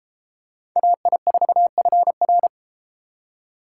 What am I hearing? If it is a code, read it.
Morse code, heard as a keyed tone, about 30 wpm, 700 Hz. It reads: AI4FR